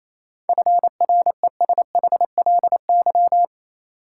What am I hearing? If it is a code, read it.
Morse code, heard as a keyed tone, about 28 wpm, 700 Hz. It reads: FRESHLY
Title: Sets of 5 Words Encore - New Word Drill - 28wpm